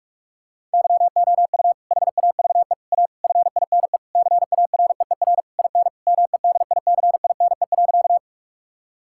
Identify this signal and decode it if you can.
Morse code, heard as a keyed tone, about 46 wpm, 690 Hz. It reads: YOU HAVE A FINE CAREER IN MEDICINE.
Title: Sentences from Top 1000 Words - Review Part 6 - 45wpm